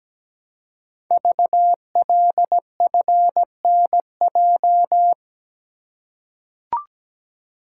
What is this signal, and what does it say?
Morse code, heard as a keyed tone, about 17 wpm, 700 Hz. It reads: VLFNJ
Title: Code Groups - 5 Letters - 17wpm